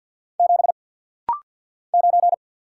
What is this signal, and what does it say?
Morse code, heard as a keyed tone, about 50 wpm, 700 Hz. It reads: 7 9
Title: Single Letter-Number - Rapid-Fire 50wpm